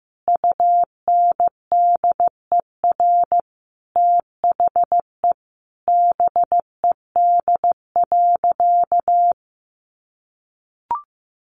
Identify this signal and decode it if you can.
Morse code, heard as a keyed tone, about 15 wpm, 700 Hz. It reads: UNDER THE BED.
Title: Easy Phrases 15wpm